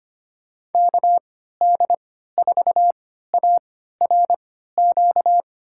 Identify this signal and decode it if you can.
Morse code, heard as a keyed tone, about 25 wpm, spaced as for 14 wpm, 700 Hz. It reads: KD4ARQ
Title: USA Call Signs - 6 Characters - 3x Character Spacing - 25wpm